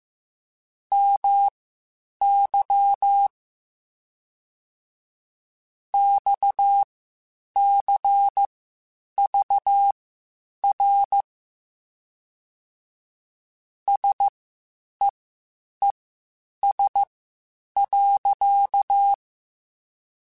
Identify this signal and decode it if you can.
Morse code, heard as a keyed tone, about 15 wpm, spaced as for 8 wpm, 800 Hz. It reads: MY XCVR SEES.
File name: QSOs / QSOs-mp3s / WD0FBT 1@13